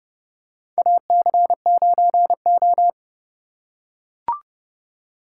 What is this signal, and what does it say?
Morse code, heard as a keyed tone, about 32 wpm, 700 Hz. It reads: AC9O
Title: USA Call Signs - 4 Characters - 2x Character Spacing - 30wpm